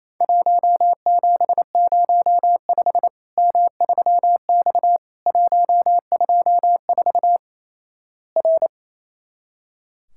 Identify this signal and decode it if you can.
Morse code, heard as a keyed tone, about 28 wpm, 700 Hz. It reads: 1705 M3X 124 R
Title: Worked All Europe DX Contest - QTC Exchange - 28wpm